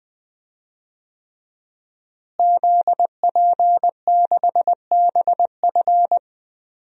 Morse code, heard as a keyed tone, about 20 wpm, 700 Hz. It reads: ZP6BF
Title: DXCC-100 Call Signs - 5 Characters - Repeated 3x - 20wpm